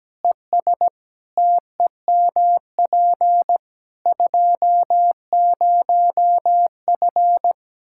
Morse code, heard as a keyed tone, about 17 wpm, 700 Hz. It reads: ES TEMP 20F
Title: Practice QSO - Exchange 2 - Info - 17wpm